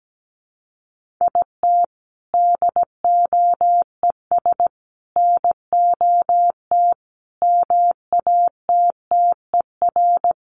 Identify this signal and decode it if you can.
Morse code, heard as a keyed tone, about 17 wpm, 700 Hz. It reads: IT DOES NOT MATTER
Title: Sets of 5 Words Encore 17wpm